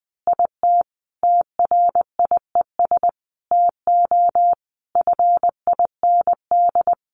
Morse code, heard as a keyed tone, about 20 wpm, 700 Hz. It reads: IT TRIES TO FIND
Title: Additional Easy Phrases 20wpm